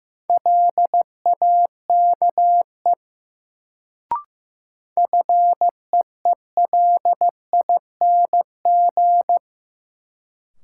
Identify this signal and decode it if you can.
Morse code, heard as a keyed tone, about 15 wpm, 700 Hz. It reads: LAKE FEELING
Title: Sentences from Top 1000 Words - Review Part 3 - New Word Drill - 15wpm